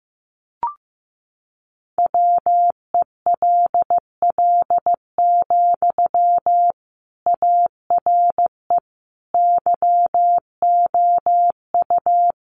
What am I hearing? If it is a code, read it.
Morse code, heard as a keyed tone, about 15 wpm, 700 Hz. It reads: WELL, ARE YOU
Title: Sentences from Top 800 Words 15wpm